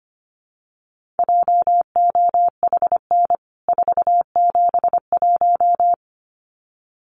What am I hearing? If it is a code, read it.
Morse code, heard as a keyed tone, about 25 wpm, 700 Hz. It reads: JOHN 471
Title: CWT Exchange 25wpm